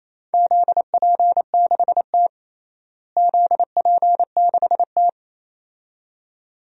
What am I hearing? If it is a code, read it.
Morse code, heard as a keyed tone, about 28 wpm, 700 Hz. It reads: ZP6T ZP6T
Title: DXCC-100 Call Signs - 4 Characters - Repeated 2x - 28wpm